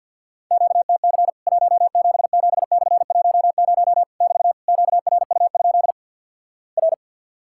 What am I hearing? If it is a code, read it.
Morse code, heard as a keyed tone, about 50 wpm, 700 Hz. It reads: QTC 177/10 <BT> QRU? R
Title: Worked All Europe DX Contest - QTC Exchange - 50wpm